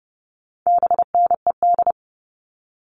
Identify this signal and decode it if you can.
Morse code, heard as a keyed tone, about 30 wpm, 700 Hz. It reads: BNED
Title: All Cut Numbers - 4-Digit 30wpm